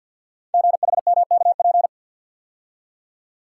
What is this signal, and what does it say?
Morse code, heard as a keyed tone, about 49 wpm, 700 Hz. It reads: GHMKP
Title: Code Groups - 5 Letters - 50wpm